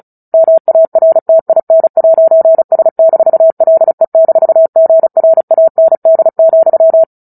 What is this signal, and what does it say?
Morse code, heard as a keyed tone, about 34 wpm, 650 Hz. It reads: MARTIN'S-LE-GRAND,
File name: Book Two / 09-Chapter 9 - 35 WPM